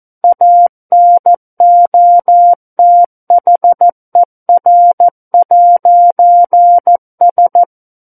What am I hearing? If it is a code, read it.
Morse code, heard as a keyed tone, about 14 wpm, 700 Hz. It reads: ANOTHER'S
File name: Book One / 05-Chapter 5 - 14 WPM